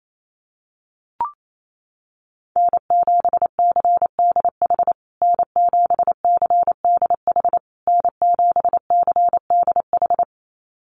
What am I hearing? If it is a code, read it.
Morse code, heard as a keyed tone, about 28 wpm, 700 Hz. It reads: N7CDH N7CDH N7CDH
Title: USA Call Signs Repeated 3x 28wpm